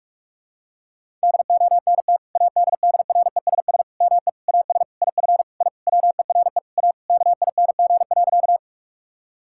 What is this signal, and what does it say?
Morse code, heard as a keyed tone, about 46 wpm, 700 Hz. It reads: DONT ADDRESS ME AS IF I WERE A KING.
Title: Sentences from Top 1000 Words - Review Part 6 - 45wpm